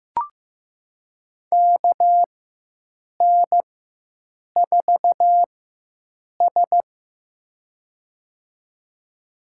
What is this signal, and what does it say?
Morse code, heard as a keyed tone, about 15 wpm, spaced as for 7 wpm, 700 Hz. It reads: KN4S
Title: USA Call Signs - 4 Characters - 4x Character Spacing - 15wpm